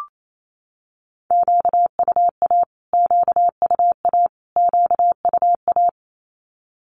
Morse code, heard as a keyed tone, about 28 wpm, 700 Hz. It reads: QUA QUA QUA
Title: Q-Codes Repeated 3x 28wpm